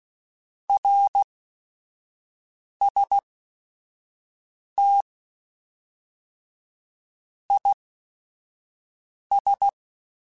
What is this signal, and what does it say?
Morse code, heard as a keyed tone, about 16 wpm, spaced as for 5 wpm, 800 Hz. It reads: RST IS